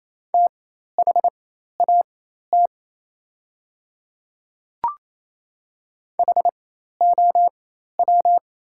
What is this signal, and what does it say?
Morse code, heard as a keyed tone, about 28 wpm, spaced as for 13 wpm, 700 Hz. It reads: THAT HOW
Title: 4x Character Spacing - Top 100 Words - 28wpm